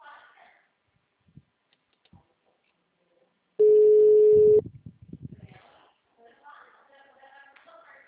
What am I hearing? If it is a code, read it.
Ringback tone (Europe, 425 Hz), with speech.